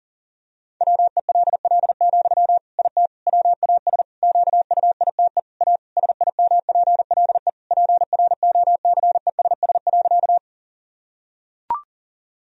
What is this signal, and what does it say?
Morse code, heard as a keyed tone, about 40 wpm, 700 Hz. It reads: WELL, IT WAS QUITE A SIMPLE PROCESS.
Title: Sentences from Top 1000 Words - Review Part 3 - 40wpm